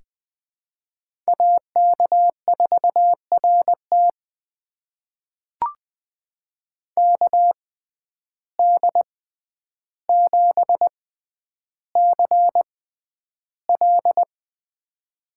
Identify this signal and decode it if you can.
Morse code, heard as a keyed tone, about 20 wpm, spaced as for 10 wpm, 700 Hz. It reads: AK4RT KD7CL
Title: USA Call Signs - 5 Characters - 6x Character Spacing - 20wpm